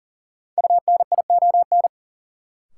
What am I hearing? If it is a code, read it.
Morse code, heard as a keyed tone, about 40 wpm, 700 Hz. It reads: UNION
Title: Sentences from Top 1400 Words - New Word Drill Alt - 40wpm